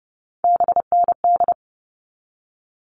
Morse code, heard as a keyed tone, about 30 wpm, 700 Hz. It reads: BND
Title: All Cut Numbers - 3-Digit 30wpm